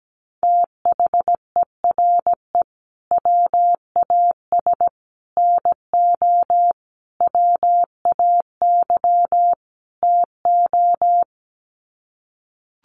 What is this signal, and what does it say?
Morse code, heard as a keyed tone, about 17 wpm, 700 Hz. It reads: THERE WAS NO WAY TO
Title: Sets of 5 Words Encore 17wpm